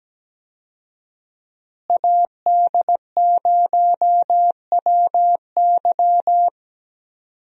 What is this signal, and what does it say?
Morse code, heard as a keyed tone, about 17 wpm, 700 Hz. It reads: AD0WY
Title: USA Call Signs - 5 Characters - Repeated 3x - 17wpm